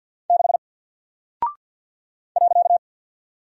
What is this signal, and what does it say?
Morse code, heard as a keyed tone, about 50 wpm, 700 Hz. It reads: 6 .
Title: Single Character (Letter:Number:Punctuation:Prosign) Alt 50wpm